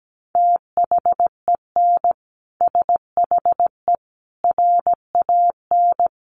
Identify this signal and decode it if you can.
Morse code, heard as a keyed tone, about 17 wpm, 700 Hz. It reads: THEN SHE RAN
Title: Sentences from Top 1400 Words 17wpm